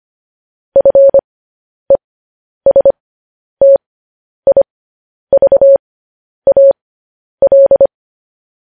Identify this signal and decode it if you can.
Morse code, heard as a keyed tone, about 25 wpm, spaced as for 10 wpm, 550 Hz. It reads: FESTIVAL